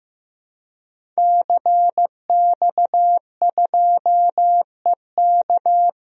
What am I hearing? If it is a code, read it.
Morse code, heard as a keyed tone, about 15 wpm, 700 Hz. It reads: CX2EK
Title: DXCC-100 Call Signs - 5 Characters - 15wpm